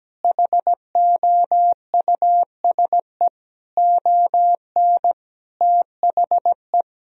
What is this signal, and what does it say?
Morse code, heard as a keyed tone, about 17 wpm, 700 Hz. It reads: HOUSE ON THE
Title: Sentences from Top 1000 Words - Review Part 1 - 17wpm